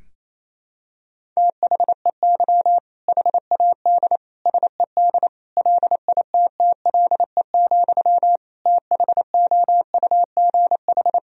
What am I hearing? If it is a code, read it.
Morse code, heard as a keyed tone, about 28 wpm, 700 Hz. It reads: THEY HAD SED LITTLE, THOUGH
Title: Sentences from Top 400 Words and Abbreviations 28wpm